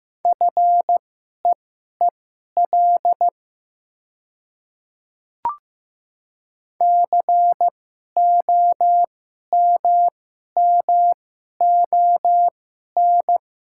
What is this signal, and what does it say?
Morse code, heard as a keyed tone, about 15 wpm, spaced as for 11 wpm, 700 Hz. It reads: FEEL COMMON
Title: 2x Character Spacing - Top 500 Words - 15wpm